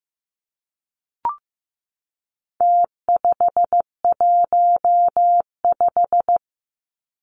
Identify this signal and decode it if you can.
Morse code, heard as a keyed tone, about 15 wpm, 700 Hz. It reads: T515
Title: Numbers - Contest Style - 4-Digit 15wpm